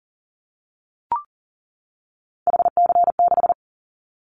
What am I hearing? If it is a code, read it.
Morse code, heard as a keyed tone, about 39 wpm, 700 Hz. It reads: HC6
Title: Callsign Prefixes for Top 100 DXCC - 40wpm